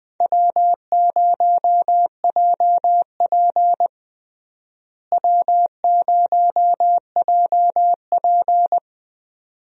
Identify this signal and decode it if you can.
Morse code, heard as a keyed tone, about 20 wpm, 700 Hz. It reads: W0JP W0JP
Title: USA Call Signs - 4 Characters - Repeated 2x - 20wpm